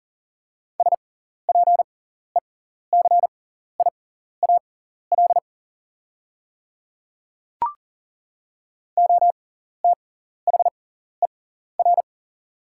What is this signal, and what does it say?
Morse code, heard as a keyed tone, about 40 wpm, spaced as for 14 wpm, 700 Hz. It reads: SPECIAL OTHER
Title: 6x Character Spacing - Top 500 Words - 40wpm